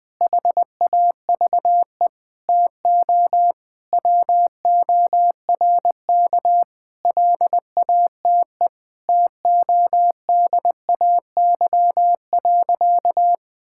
Morse code, heard as a keyed tone, about 21 wpm, 700 Hz. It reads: HAVE TO WORK LATE TODAY.
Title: Sentences with 5-Word Phrases - 20wpm